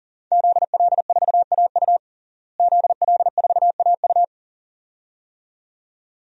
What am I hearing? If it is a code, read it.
Morse code, heard as a keyed tone, about 40 wpm, 690 Hz. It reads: ZL4AU ZL4AU
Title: DXCC-100 Call Signs - 5 Characters - Repeated 2x - 40wpm